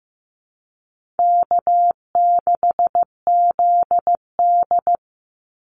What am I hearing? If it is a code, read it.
Morse code, heard as a keyed tone, about 15 wpm, 700 Hz. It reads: K6ZD
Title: Easy Call Signs Repeated 2x - 15wpm